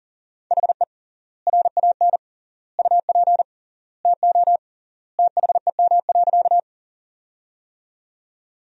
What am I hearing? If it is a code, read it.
Morse code, heard as a keyed tone, about 37 wpm, 700 Hz. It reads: HE RAN UP TO THEM.
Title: 3x Word-Spacing - Sentences from Top 500 Words - 40wpm